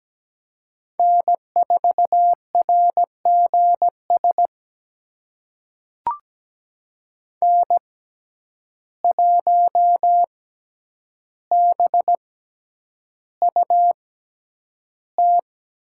Morse code, heard as a keyed tone, about 17 wpm, spaced as for 9 wpm, 700 Hz. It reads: N4RGS N1BUT
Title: USA Call Signs - 5 Characters - 6x Character Spacing - 17wpm